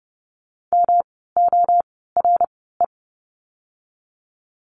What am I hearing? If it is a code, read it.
Morse code, heard as a keyed tone, about 30 wpm, spaced as for 17 wpm, 700 Hz. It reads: MORE
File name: course / 30:10wpm / 230 Lesson 37f - Letter x - Words 30wpm